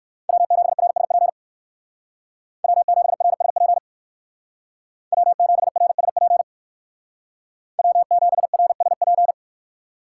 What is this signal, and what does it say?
Morse code, heard as a keyed tone, about 64 wpm, 700 Hz. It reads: W7RSP W7RSP W7RSP W7RSP